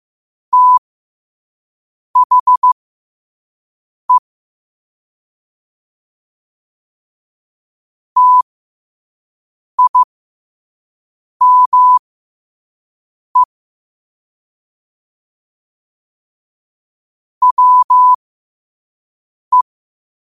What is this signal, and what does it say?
Morse code, heard as a keyed tone, about 15 wpm, spaced as for 5 wpm, 1000 Hz. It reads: THE TIME WE
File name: qotdfiles / 5wpm